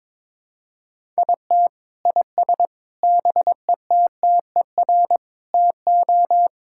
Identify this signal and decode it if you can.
Morse code, heard as a keyed tone, about 22 wpm, 700 Hz. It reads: IT IS BETTER TO
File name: Sentences with 7-Word Phrases - 22wpm